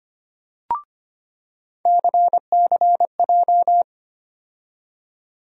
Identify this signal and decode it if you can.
Morse code, heard as a keyed tone, about 25 wpm, 700 Hz. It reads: CCJ
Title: USA Call Sign Postfixes - 3 Letters 25wpm